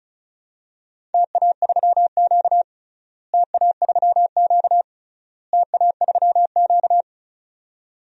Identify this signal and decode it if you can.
Morse code, heard as a keyed tone, about 36 wpm, 700 Hz. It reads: TA3Q TA3Q TA3Q
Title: DXCC-100 Call Signs - 4 Characters - Repeated 3x - 35wpm